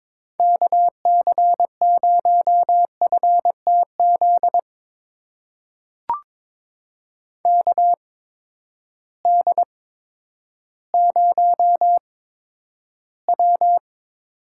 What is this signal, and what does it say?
Morse code, heard as a keyed tone, about 22 wpm, spaced as for 11 wpm, 700 Hz. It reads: KC0FTZ KD0W
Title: USA Call Signs - 6 Characters - 8x Character Spacing - 22wpm